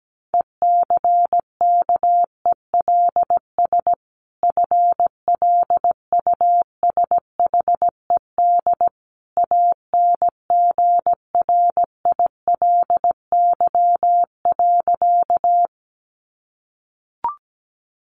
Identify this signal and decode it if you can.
Morse code, heard as a keyed tone, about 17 wpm, 700 Hz. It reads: ECKELS FLUSHED ANGRILY.
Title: A Sound Of Thunder 17wpm